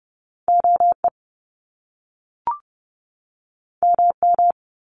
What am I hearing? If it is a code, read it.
Morse code, heard as a keyed tone, about 30 wpm, 700 Hz. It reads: OE MM
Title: Top Two Letter Combinations in Words 30wpm